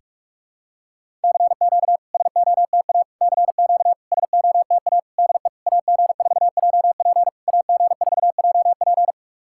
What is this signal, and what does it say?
Morse code, heard as a keyed tone, about 44 wpm, 700 Hz. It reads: CQ SOTA CQ SOTA DE AG4JP AG4JP
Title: SOTA QSO Practice 45wpm